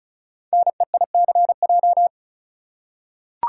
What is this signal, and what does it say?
Morse code, heard as a keyed tone, about 36 wpm, 700 Hz. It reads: NEICJ
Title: Code Groups - 5 Letters - 35wpm